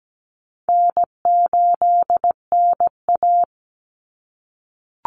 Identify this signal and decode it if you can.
Morse code, heard as a keyed tone, about 17 wpm, 700 Hz. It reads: N8NA
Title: Call Signs - 17wpm